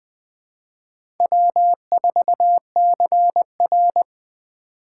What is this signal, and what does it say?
Morse code, heard as a keyed tone, about 20 wpm, 700 Hz. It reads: W4CR
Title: USA Call Signs - 6x Character Spacing - 20wpm